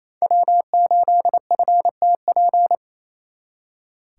Morse code, heard as a keyed tone, about 35 wpm, 700 Hz. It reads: W8FTP
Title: Speed Racing - USA 1x3 Call Signs 28wpm